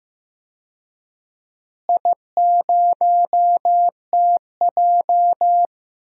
Morse code, heard as a keyed tone, about 15 wpm, 700 Hz. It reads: I0TJ
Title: DXCC-100 Call Signs - 4 Characters - Repeated 3x - 15wpm